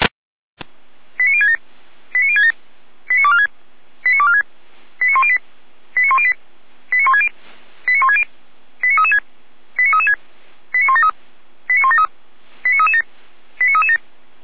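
Five-tone selective call (ZVEI), heard as quick runs of five tones, tones of about 70 ms. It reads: 89067 89067 89246 89246 89108 89108 89160 89160 89336 89336 89173 89173 89337 89337
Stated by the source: ZVEI five-tone call received via SDR.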